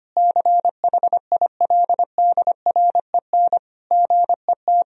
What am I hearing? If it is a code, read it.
Morse code, heard as a keyed tone, about 25 wpm, 700 Hz. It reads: CHILDREN GET
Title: Sentences from Top 1000 Words - Review Part 1 - 25wpm